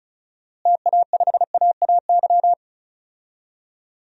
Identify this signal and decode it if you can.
Morse code, heard as a keyed tone, about 36 wpm, 700 Hz. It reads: TA5AAY
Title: DXCC-100 Call Signs - 6 Characters - Repeated 2x - 35wpm